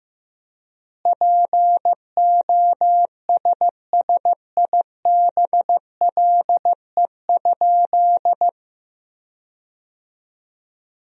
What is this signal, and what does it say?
Morse code, heard as a keyed tone, about 15 wpm, 700 Hz. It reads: POSSIBLE?
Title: 3x Word-Spacing - Sentences from Top 500 Words - 15wpm